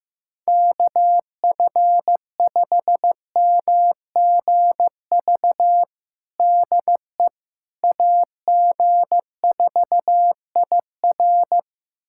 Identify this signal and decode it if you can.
Morse code, heard as a keyed tone, about 15 wpm, 700 Hz. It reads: KF5MGV DE AG4IR
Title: USA Callsign DE Callsign 15wpm